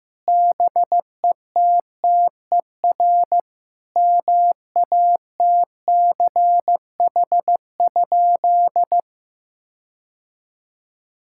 Morse code, heard as a keyed tone, about 15 wpm, 700 Hz. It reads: BETTER MATCH?
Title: Sentences from Top 1600 Words 15wpm